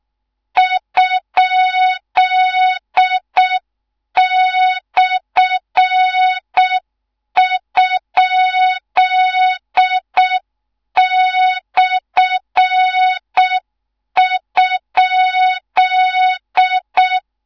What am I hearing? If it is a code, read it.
Morse code, heard as a keyed tone, about 6 wpm, 3700 Hz, with noise. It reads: ?/?/?
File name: Morze-HA2MN / hang / 13IRASJELEK1 HA2MN